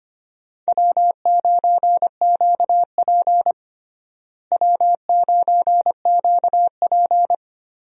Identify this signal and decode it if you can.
Morse code, heard as a keyed tone, about 25 wpm, 700 Hz. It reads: W9QP W9QP
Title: USA Call Signs - 4 Characters - Repeated 3x - 25wpm